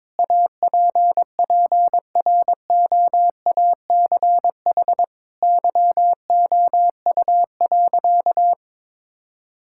Morse code, heard as a keyed tone, about 22 wpm, 700 Hz. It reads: APPROACH YOU.
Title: Sentences from Top 1000 Words - Review Part 6 - 22wpm